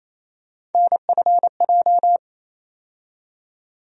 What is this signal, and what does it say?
Morse code, heard as a keyed tone, about 28 wpm, 700 Hz. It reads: NFJ
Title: USA Call Sign Postfixes - 3 Letters 28wpm